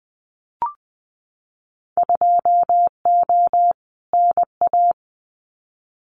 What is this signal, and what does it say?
Morse code, heard as a keyed tone, about 19 wpm, 700 Hz. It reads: 2O NA
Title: Winter Field Day Exchanges 20wpm